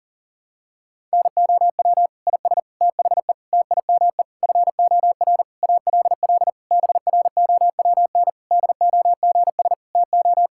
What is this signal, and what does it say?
Morse code, heard as a keyed tone, about 40 wpm, 700 Hz. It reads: NOW IS THE TIME FOR ALL BROWN DOGS TO
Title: Pangram Challenge v2 - 40wpm